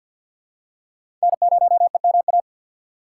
Morse code, heard as a keyed tone, about 50 wpm, 700 Hz. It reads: N0EMA
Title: USA Call Signs - 5 Characters - 2x Character Spacing - 50wpm